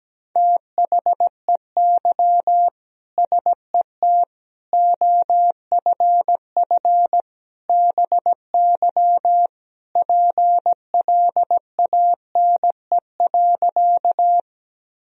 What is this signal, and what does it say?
Morse code, heard as a keyed tone, about 17 wpm, 700 Hz. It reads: THEY SET OFF BY PLANE.
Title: Sentences from Top 1000 Words - Review Part 3 - 17wpm